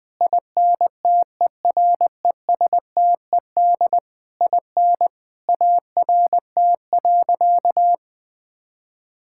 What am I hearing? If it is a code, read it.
Morse code, heard as a keyed tone, about 20 wpm, 700 Hz. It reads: INTERESTED IN ART.
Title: Sentences from Top 1000 Words - Review Part 6 - 20wpm